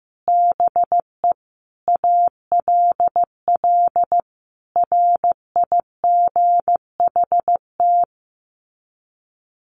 Morse code, heard as a keyed tone, about 15 wpm, 700 Hz. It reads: BE ALL RIGHT
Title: Sets of 5 Words Encore2 15wpm